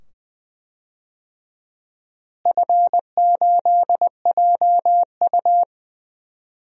Morse code, heard as a keyed tone, about 20 wpm, 700 Hz. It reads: F8JU
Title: DXCC-100 Call Signs - 4 Characters - Repeated 2x - 20wpm